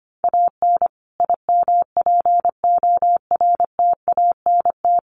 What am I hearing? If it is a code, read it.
Morse code, heard as a keyed tone, about 25 wpm, 700 Hz. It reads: AN IMPORTANT
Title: Sets of 5 Words Encore2 25wpm